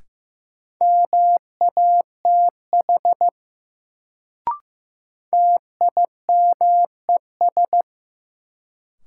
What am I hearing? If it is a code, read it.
Morse code, heard as a keyed tone, about 15 wpm, 700 Hz. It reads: MATH TIMES
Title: Sentences from Top 1000 Words - Review Part 6 - New Word Drill - 15wpm